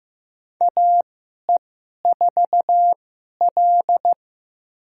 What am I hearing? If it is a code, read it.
Morse code, heard as a keyed tone, about 16 wpm, spaced as for 11 wpm, 700 Hz. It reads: AE4L
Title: USA Call Signs - 4 Characters - 2x Character Spacing - 15wpm